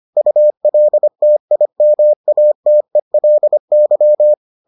Morse code, heard as a keyed tone, about 25 wpm, 600 Hz. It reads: ULTIMATELY